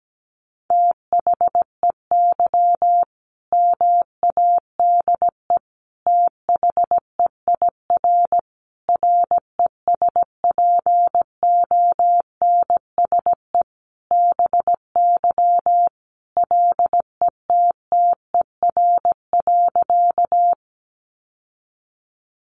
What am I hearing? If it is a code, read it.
Morse code, heard as a keyed tone, about 17 wpm, 700 Hz. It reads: THEY MADE THEIR RESPONSE BY LETTER.
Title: Sentences from Top 1400 Words 17wpm